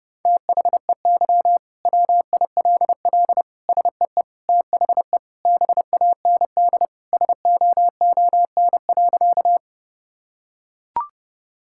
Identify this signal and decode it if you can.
Morse code, heard as a keyed tone, about 30 wpm, 700 Hz. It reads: THEY WILL SEE THE BAND SOON.
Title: Sentences from Top 1600 Words 30wpm